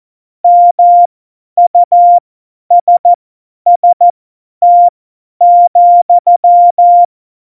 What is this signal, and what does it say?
Morse code, heard as a keyed tone, about 14 wpm, spaced as for 10 wpm, 700 Hz. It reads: MUSST,
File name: Morsekurs / Witze / Witz 24.Dezember 2019, 50 F70